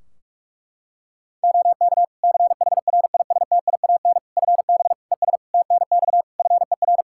Morse code, heard as a keyed tone, about 46 wpm, 700 Hz. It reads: OK CHRISTIAN FB ES TNX FER
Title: Practice QSO - Exchange 2 - Info - 45wpm